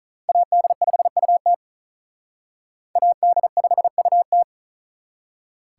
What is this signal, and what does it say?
Morse code, heard as a keyed tone, about 45 wpm, 700 Hz. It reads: AD5UT AD5UT